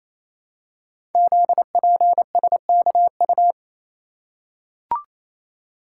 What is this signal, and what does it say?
Morse code, heard as a keyed tone, about 29 wpm, 700 Hz. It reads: ZPSKU
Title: Code Groups - 5 Letter:Numbers - 28wpm